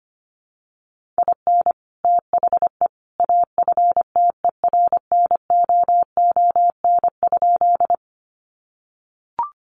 Morse code, heard as a keyed tone, about 25 wpm, 700 Hz. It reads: IN THE AFTERNOON?
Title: Sentences from Top 1500 Words 25wpm